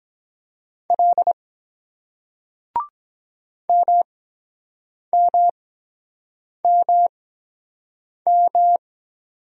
Morse code, heard as a keyed tone, about 23 wpm, spaced as for 7 wpm, 700 Hz. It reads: L MMMM